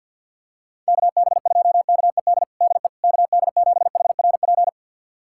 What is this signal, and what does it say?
Morse code, heard as a keyed tone, about 51 wpm, 700 Hz. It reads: KB2KED DE KD7HRP
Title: USA Callsign DE Callsign 50wpm